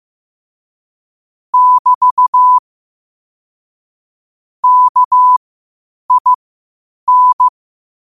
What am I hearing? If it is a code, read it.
Morse code, heard as a keyed tone, about 15 wpm, spaced as for 8 wpm, 1000 Hz. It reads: <BT> KIN
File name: qotdfiles / 8wpm